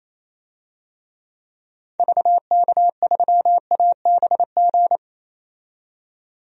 Morse code, heard as a keyed tone, about 28 wpm, 700 Hz. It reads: VK3ABG
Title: DXCC-100 Call Signs - 6 Characters - 28wpm